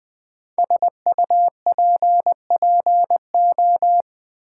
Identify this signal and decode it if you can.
Morse code, heard as a keyed tone, about 20 wpm, 700 Hz. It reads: SUPPO
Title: Build A Word - 5 Letters At A Time - 10-Letter Words - 20wpm